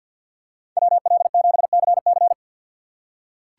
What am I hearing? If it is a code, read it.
Morse code, heard as a keyed tone, about 48 wpm, 700 Hz. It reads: WL7CC
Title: USA Call Signs - 5 Characters - 4x Character Spacing - 50wpm